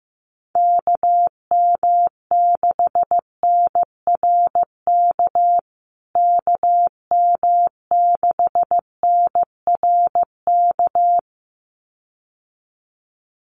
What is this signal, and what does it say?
Morse code, heard as a keyed tone, about 15 wpm, 700 Hz. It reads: KM6NRK KM6NRK
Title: USA Call Signs Repeated 2x 15wpm